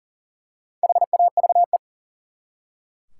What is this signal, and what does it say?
Morse code, heard as a keyed tone, about 40 wpm, 700 Hz. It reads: HAVE
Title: Top 180 QSO Elements 40wpm